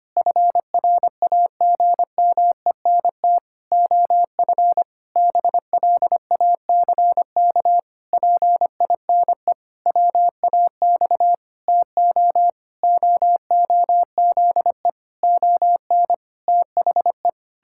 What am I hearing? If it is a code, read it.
Morse code, heard as a keyed tone, about 25 wpm, 700 Hz. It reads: FRAGMENT OF BLACK PINE WAX TO OOZE ON THE
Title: Pangram Challenge v2 - 25wpm